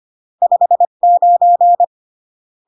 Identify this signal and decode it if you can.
Morse code, heard as a keyed tone, about 25 wpm, 700 Hz. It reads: 59